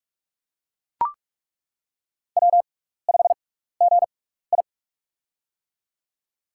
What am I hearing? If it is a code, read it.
Morse code, heard as a keyed tone, about 41 wpm, spaced as for 15 wpm, 700 Hz. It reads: W5GI
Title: USA Call Signs - 4 Characters - 6x Character Spacing - 45wpm